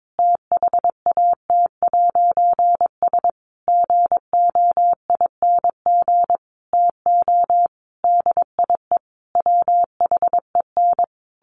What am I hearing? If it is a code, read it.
Morse code, heard as a keyed tone, about 22 wpm, 700 Hz. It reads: THAT'S GOING TO DIE WHEN
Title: A Sound Of Thunder 22wpm